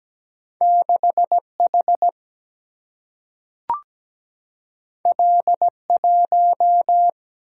Morse code, heard as a keyed tone, about 17 wpm, 700 Hz. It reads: 6 H L 1
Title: Single-Letter Single-Number 17wpm